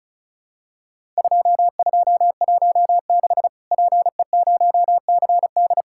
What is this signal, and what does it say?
Morse code, heard as a keyed tone, about 36 wpm, 700 Hz. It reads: 2216 PE0CD
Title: Worked All Europe DX Contest - QTC Exchange - 35wpm